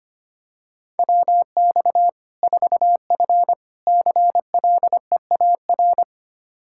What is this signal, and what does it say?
Morse code, heard as a keyed tone, about 25 wpm, 700 Hz. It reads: WX 4F CLEAR
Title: Practice QSO - Exchange 2 - Info - 25wpm